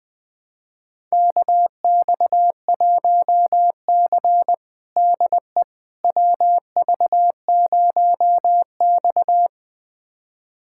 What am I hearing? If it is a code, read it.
Morse code, heard as a keyed tone, about 20 wpm, 700 Hz. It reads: KX1C DE WV0X
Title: USA Callsign DE Callsign - 4 Characters - 20wpm